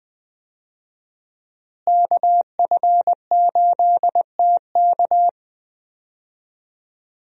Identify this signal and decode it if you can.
Morse code, heard as a keyed tone, about 20 wpm, 700 Hz. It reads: KF8TK
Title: USA Call Signs - 5 Characters - Repeated 3x - 20wpm